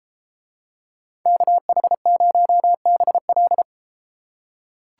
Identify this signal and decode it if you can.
Morse code, heard as a keyed tone, about 33 wpm, 700 Hz. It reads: KH0BL